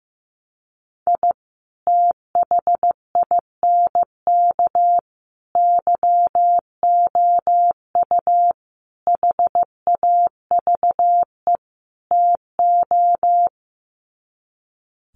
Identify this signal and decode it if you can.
Morse code, heard as a keyed tone, about 15 wpm, 700 Hz. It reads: I THINK YOU HAVE TO
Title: Sets of 5 Words 15wpm